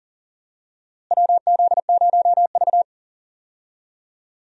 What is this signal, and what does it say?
Morse code, heard as a keyed tone, about 40 wpm, 700 Hz. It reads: WZ0V
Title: USA Call Signs - 4 Characters - Repeated 3x - 40wpm